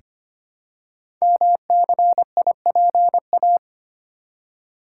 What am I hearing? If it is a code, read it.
Morse code, heard as a keyed tone, about 25 wpm, 700 Hz. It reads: MCIPA
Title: Code Groups - 5 Letters - 25wpm